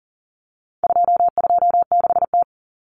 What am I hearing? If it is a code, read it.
Morse code, heard as a keyed tone, about 40 wpm, 700 Hz. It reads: 226T
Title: Common Cut Numbers - 4-Digit 40wpm